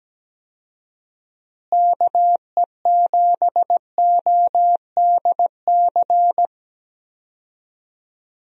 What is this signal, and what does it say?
Morse code, heard as a keyed tone, about 17 wpm, 700 Hz. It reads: KE7ODC
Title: USA Call Signs - 6 Characters - Repeated 3x - 17wpm